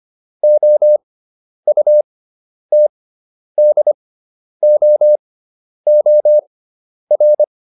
Morse code, heard as a keyed tone, about 25 wpm, spaced as for 10 wpm, 600 Hz. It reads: OUTDOOR